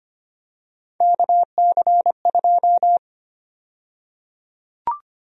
Morse code, heard as a keyed tone, about 25 wpm, 700 Hz. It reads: KC2
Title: USA Call Sign Prefixes - 25wpm